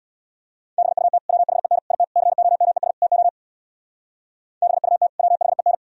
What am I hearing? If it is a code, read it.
Morse code, heard as a keyed tone, about 64 wpm, 690 Hz. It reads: BUT WHEN IT COMES TO BUT WHEN